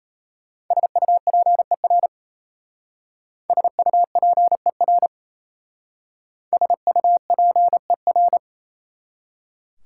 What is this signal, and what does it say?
Morse code, heard as a keyed tone, about 35 wpm, 700 Hz. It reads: SUPER SUPER SUPER
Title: Speed-Racing Prefixes and Suffixes 28wpm